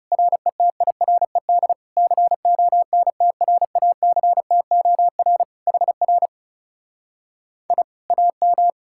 Morse code, heard as a keyed tone, about 40 wpm, 700 Hz. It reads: RETIRED CONTRACTOR HR I AM